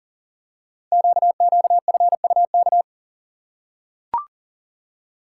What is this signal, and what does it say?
Morse code, heard as a keyed tone, about 39 wpm, 690 Hz. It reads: QQFUK
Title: Code Groups - 5 Letters - 40wpm